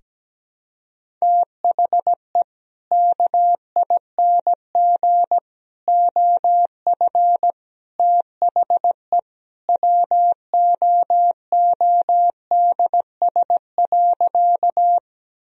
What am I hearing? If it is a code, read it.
Morse code, heard as a keyed tone, about 17 wpm, 700 Hz. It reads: THE KING OF THE WOODS.
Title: Sentences from Top 1000 Words - Review Part 2 - 17wpm